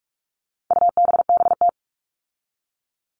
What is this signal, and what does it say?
Morse code, heard as a keyed tone, about 44 wpm, 700 Hz. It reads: UBBT
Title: All Cut Numbers - 4-Digit 45wpm